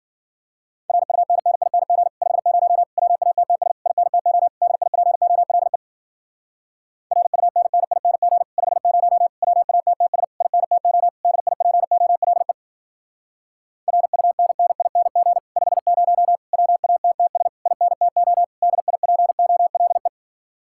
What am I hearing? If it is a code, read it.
Morse code, heard as a keyed tone, about 52 wpm, 700 Hz. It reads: RUNNING 50 WATTS INTO DIPOLE RUNNING 50 WATTS INTO DIPOLE RUNNING 50 WATTS INTO DIPOLE